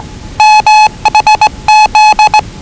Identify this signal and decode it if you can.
Morse code, heard as a keyed tone, about 20 wpm, 840 Hz. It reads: MHZ
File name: WellingtonRadioZlwCloseDownOnHfAndMf / 01-Zlw 4277khz 417.5 qrt until 0925z